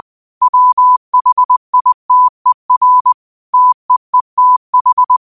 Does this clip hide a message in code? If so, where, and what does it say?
Yes: Morse code from 0.4 to 5.2 s, heard as a keyed tone, about 20 wpm, 1000 Hz. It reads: WHITER TEETH